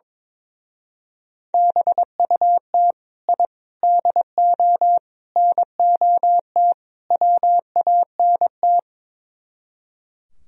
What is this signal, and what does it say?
Morse code, heard as a keyed tone, about 22 wpm, 700 Hz. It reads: BUT I DO NOT WANT
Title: Speed-Racing - Sets of 5 Words 22wpm